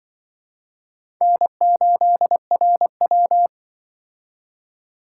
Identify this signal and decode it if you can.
Morse code, heard as a keyed tone, about 25 wpm, 700 Hz. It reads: N8RW
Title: Speed Racing - USA Call Signs 20wpm